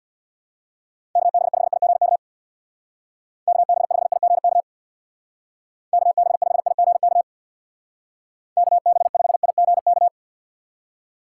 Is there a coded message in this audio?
Yes: Morse code from 1.2 to 10.1 s, heard as a keyed tone, about 65 wpm, 700 Hz. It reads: KB5IGK KB5IGK KB5IGK KB5IGK